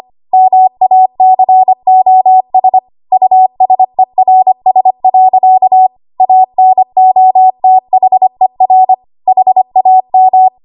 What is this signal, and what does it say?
Morse code, heard as a keyed tone, about 25 wpm, 750 Hz. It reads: MACOS USERS. ANOTHER HAM
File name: Archive / 25 WPM / 25WPM